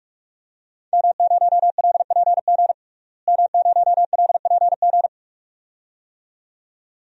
Morse code, heard as a keyed tone, about 46 wpm, 700 Hz. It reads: M0LPG M0LPG
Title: DXCC-100 Call Signs - 5 Characters - Repeated 2x - 45wpm